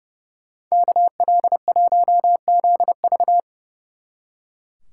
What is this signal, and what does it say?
Morse code, heard as a keyed tone, about 37 wpm, 700 Hz. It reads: KL1ZV
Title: Speed Racing - USA 2x2 Call Signs 30wpm